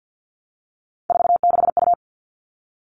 Morse code, heard as a keyed tone, about 50 wpm, 700 Hz. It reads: 46U
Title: All Cut Numbers - 3-Digit 50wpm